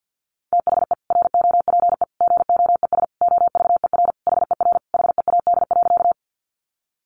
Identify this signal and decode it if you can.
Morse code, heard as a keyed tone, about 50 wpm, 700 Hz. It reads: THE ROPE GOES OVER HER HEAD.
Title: Easy Phrases 50wpm